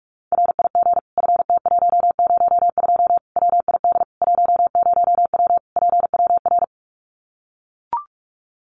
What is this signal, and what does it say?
Morse code, heard as a keyed tone, about 44 wpm, 700 Hz. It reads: RIG FT102 WID 10W PWR
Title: Common QSO Phrases 45wpm